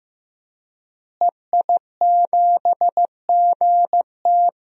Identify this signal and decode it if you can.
Morse code, heard as a keyed tone, about 15 wpm, 700 Hz. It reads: EI7GT
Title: DXCC-100 Call Signs - 5 Characters - Repeated 3x - 15wpm